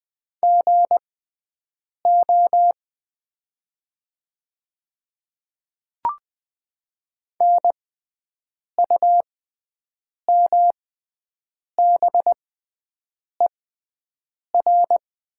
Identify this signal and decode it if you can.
Morse code, heard as a keyed tone, about 20 wpm, spaced as for 7 wpm, 700 Hz. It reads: GO NUMBER
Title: 6x Character Spacing - Top 100 Words - 20wpm